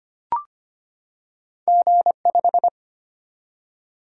Morse code, heard as a keyed tone, about 25 wpm, 700 Hz. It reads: G5
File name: Two Letter-Number Combinations - 25wpm